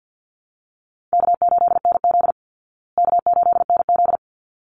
Morse code, heard as a keyed tone, about 48 wpm, 700 Hz. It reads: K8NZ K8NZ
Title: Easy Call Signs Repeated 2x - 50wpm